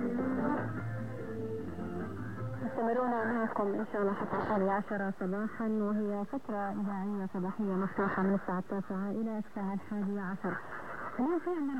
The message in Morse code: none